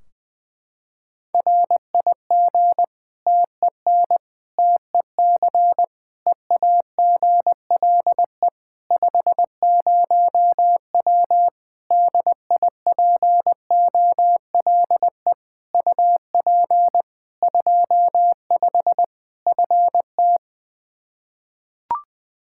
Morse code, heard as a keyed tone, about 21 wpm, 700 Hz. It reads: RIG TEN TEC EAGLE 50W DIPOLE UP 25 FT
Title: Practice QSO - Exchange 2 - Info - 20wpm